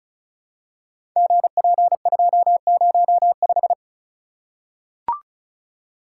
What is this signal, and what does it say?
Morse code, heard as a keyed tone, about 36 wpm, 700 Hz. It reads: GP205
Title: Code Groups - 5 Letter:Numbers - 35wpm